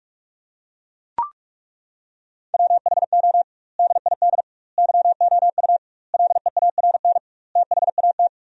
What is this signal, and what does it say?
Morse code, heard as a keyed tone, about 42 wpm, 700 Hz. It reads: WHO DID YOU LEARN THAT
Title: 2x Word-Spacing - Sentences from Top 300 Words - 45wpm